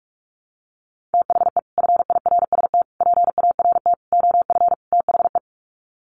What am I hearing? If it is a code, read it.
Morse code, heard as a keyed tone, about 44 wpm, 700 Hz. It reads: THE FIRST PART OF THE
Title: Sets of 5 Words Encore 45wpm